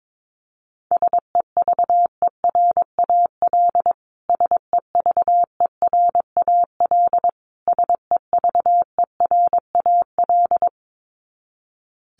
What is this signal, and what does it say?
Morse code, heard as a keyed tone, about 22 wpm, 700 Hz. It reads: SEVERAL SEVERAL SEVERAL
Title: Top 500 Words Repeated 22wpm